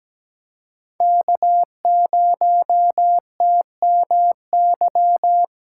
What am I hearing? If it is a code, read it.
Morse code, heard as a keyed tone, about 17 wpm, 700 Hz. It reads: K0TMY
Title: USA Callsign DE Callsign 17wpm